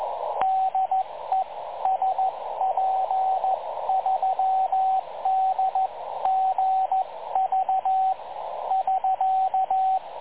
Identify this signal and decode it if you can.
Morse code, heard as a keyed tone, about 15 wpm, 750 Hz, with radio noise. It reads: DE SP3DGV <SK>
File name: dlazaawansowanych / nagraniazpasma / websdr recording 3565.0kHz